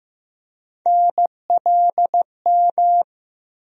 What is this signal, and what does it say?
Morse code, heard as a keyed tone, about 15 wpm, 700 Hz. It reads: NLM
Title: USA Call Sign Postfixes - 3 Letters 15wpm